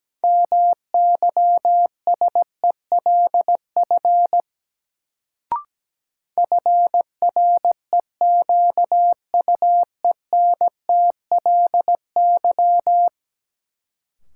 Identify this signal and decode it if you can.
Morse code, heard as a keyed tone, about 17 wpm, 700 Hz. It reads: MYSELF FREQUENTLY
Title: Sentences from Top 1300 Words - New Word Drill - 17wpm